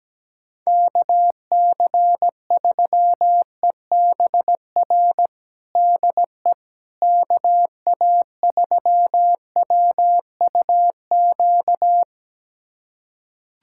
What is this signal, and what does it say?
Morse code, heard as a keyed tone, about 17 wpm, 700 Hz. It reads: KC3EBR DE KA3WUQ
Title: USA Callsign DE Callsign - 6 Characters - 17wpm